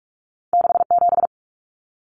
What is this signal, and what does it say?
Morse code, heard as a keyed tone, about 46 wpm, 700 Hz. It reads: 67
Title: Common Cut Numbers - 2-Digit 45wpm